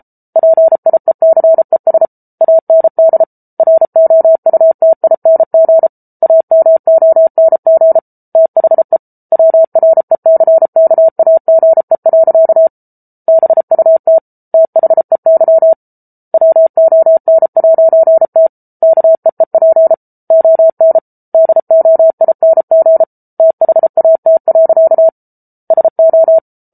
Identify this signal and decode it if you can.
Morse code, heard as a keyed tone, about 34 wpm, 650 Hz. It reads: PIECES AND ROUTING AMONG THE WRECKAGE. BUT THEY WON'T KEEP ON DOING THAT. SO